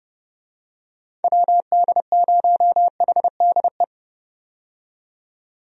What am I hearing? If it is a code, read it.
Morse code, heard as a keyed tone, about 32 wpm, 700 Hz. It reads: WD0HDE
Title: USA Call Signs - 6 Characters - 30wpm